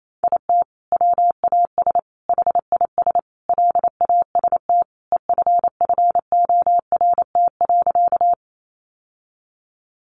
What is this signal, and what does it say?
Morse code, heard as a keyed tone, about 28 wpm, 700 Hz. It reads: IT WAS HIS LAST EFFORT.
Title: Sentences from Top 1200 Words 28wpm